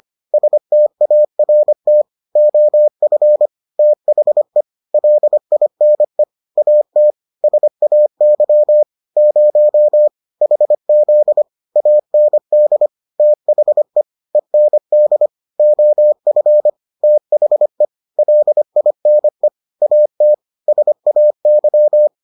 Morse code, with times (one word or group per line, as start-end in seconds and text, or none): 0.3-2.0 START
2.4-3.5 OF
3.8-4.6 THE
4.9-6.2 LINE
6.6-7.1 AT
7.4-8.8 SAY
9.2-10.1 0
10.4-11.4 HZ
11.8-12.9 AND
13.2-14.0 THE
14.3-15.3 END
15.6-16.7 OF
17.0-17.9 THE
18.2-19.5 LINE
19.8-20.3 AT
20.7-22.1 SAY